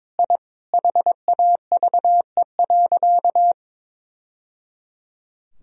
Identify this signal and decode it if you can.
Morse code, heard as a keyed tone, about 21 wpm, 700 Hz. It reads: I HAVE.
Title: Sentences from Top 1000 Words - Review Part 1 - 22wpm